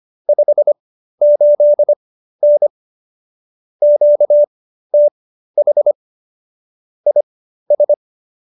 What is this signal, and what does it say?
Morse code, heard as a keyed tone, about 25 wpm, spaced as for 13 wpm, 590 Hz. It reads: 58N QTH IS
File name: qso215b 13